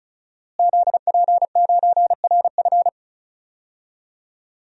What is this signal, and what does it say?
Morse code, heard as a keyed tone, about 34 wpm, 700 Hz. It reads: ZP9RF
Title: DXCC-100 Call Signs - 5 Characters - Repeated 3x - 35wpm